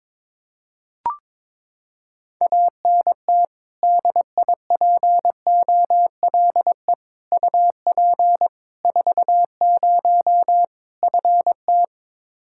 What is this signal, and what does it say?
Morse code, heard as a keyed tone, about 22 wpm, 700 Hz. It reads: ANT DIPOLE UP 40 FT
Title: Practice QSO - Exchange 2 - Info - 22wpm